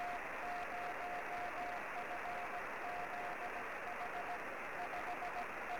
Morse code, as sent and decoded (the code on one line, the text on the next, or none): none